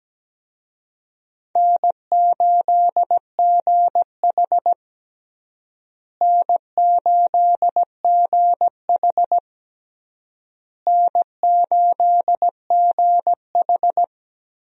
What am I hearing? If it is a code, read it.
Morse code, heard as a keyed tone, about 17 wpm, 700 Hz. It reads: N8GH N8GH N8GH
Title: USA Call Signs - 4 Characters - Repeated 3x - 17wpm